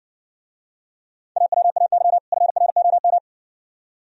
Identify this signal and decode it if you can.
Morse code, heard as a keyed tone, about 60 wpm, 700 Hz. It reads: AWAY FROM